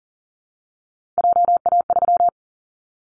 Morse code, heard as a keyed tone, about 40 wpm, 700 Hz. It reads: JA3
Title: Callsign Prefixes for Top 25 DXCC - 40wpm